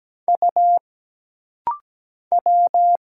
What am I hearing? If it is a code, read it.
Morse code, heard as a keyed tone, about 17 wpm, 700 Hz. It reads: U W
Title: Single Letter-Number Alt - Rapid-Fire 17wpm